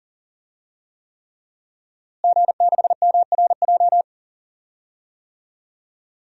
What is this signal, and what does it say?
Morse code, heard as a keyed tone, about 39 wpm, 700 Hz. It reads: G6MRJ
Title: DXCC-100 Call Signs - 5 Characters - Repeated 3x - 40wpm